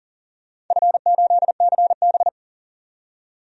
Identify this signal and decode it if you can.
Morse code, heard as a keyed tone, about 37 wpm, 700 Hz. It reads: F8CB
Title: DXCC-100 Call Signs - 4 Characters - 40wpm